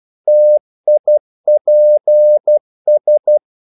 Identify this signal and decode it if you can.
Morse code, heard as a keyed tone, about 12 wpm, 600 Hz. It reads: TIPS
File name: Korte ord DK 12wpm